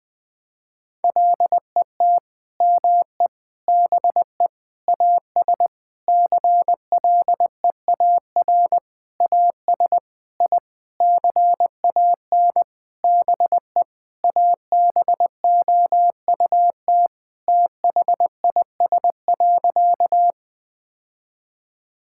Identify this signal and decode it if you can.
Morse code, heard as a keyed tone, about 20 wpm, 700 Hz. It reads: LET ME BE AS CLEAR AS I CAN BE ABOUT THIS.
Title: Sentences with 9-Word Phrases - Spoken Beginning - 20wpm